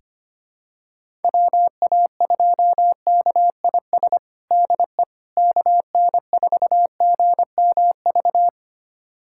Morse code, heard as a keyed tone, about 25 wpm, 700 Hz. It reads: WA2KIS DE KN4GMV
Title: USA Callsign DE Callsign 25wpm